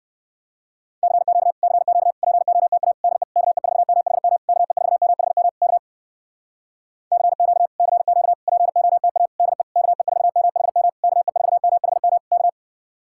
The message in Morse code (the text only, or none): CQ CQ POTA DE KE4MHM KE4MHM K CQ CQ POTA DE KE4MHM KE4MHM K